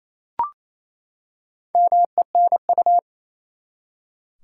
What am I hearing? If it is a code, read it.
Morse code, heard as a keyed tone, about 28 wpm, 700 Hz. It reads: MENU
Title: Four Letter Words - 28wpm